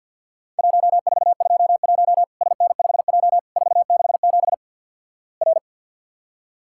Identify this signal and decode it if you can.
Morse code, heard as a keyed tone, about 48 wpm, 700 Hz. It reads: 1421 SN5J 467 R
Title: Worked All Europe DX Contest - QTC Exchange with Repeats - 50wpm